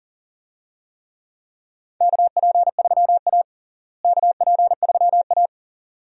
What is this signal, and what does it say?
Morse code, heard as a keyed tone, about 40 wpm, 700 Hz. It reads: KP3A KP3A
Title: USA Call Signs - 4 Characters - Repeated 2x - 40wpm